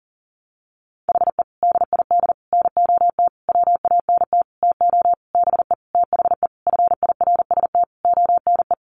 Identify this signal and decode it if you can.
Morse code, heard as a keyed tone, about 40 wpm, 700 Hz. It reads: HE DID NOT WANT TO BE THE FIRST ONE
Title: Sentences from Top 1300 Words 40wpm